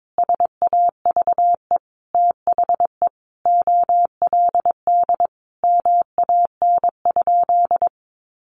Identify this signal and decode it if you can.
Morse code, heard as a keyed tone, about 22 wpm, 700 Hz. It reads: SAVE THE OLD MAN?
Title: Sentences from Top 1100 Words 22wpm